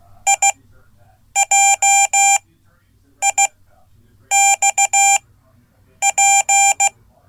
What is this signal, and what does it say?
Morse code, heard as a keyed tone, about 16 wpm, spaced as for 8 wpm, 790 Hz. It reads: IJIXP
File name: Code / 5min-b